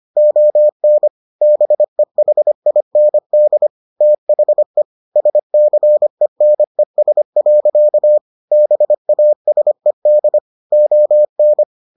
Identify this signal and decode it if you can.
Morse code, heard as a keyed tone, about 25 wpm, 600 Hz. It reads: ON BEHIND THE SCENES. BASED ON